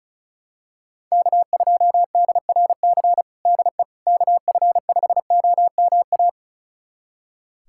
Morse code, heard as a keyed tone, about 36 wpm, 700 Hz. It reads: K2DRC DE KF5OMA
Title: USA Callsign DE Callsign 35wpm